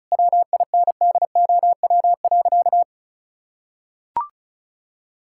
Morse code, heard as a keyed tone, about 36 wpm, 700 Hz. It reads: WINDOW.
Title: Sentences from Top 1600 Words 35wpm